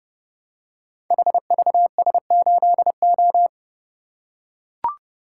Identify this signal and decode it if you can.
Morse code, heard as a keyed tone, about 30 wpm, 700 Hz. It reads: HVS8O
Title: Code Groups - 5 Letter:Numbers - 30wpm